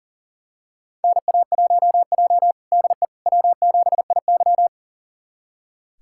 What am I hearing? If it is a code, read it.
Morse code, heard as a keyed tone, about 40 wpm, 700 Hz. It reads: NA1J DE W7IY
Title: USA Callsign DE Callsign - 4 Characters - 40wpm